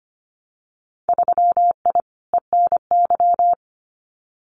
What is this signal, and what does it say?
Morse code, heard as a keyed tone, about 25 wpm, 700 Hz. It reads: 3I ENY
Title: Winter Field Day Exchanges Encore 25wpm